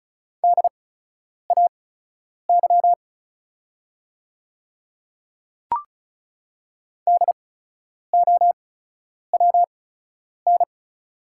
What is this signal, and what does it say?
Morse code, heard as a keyed tone, about 36 wpm, spaced as for 10 wpm, 690 Hz. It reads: DAY DOWN
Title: 8x Character Spacing - Top 100 Words - 35wpm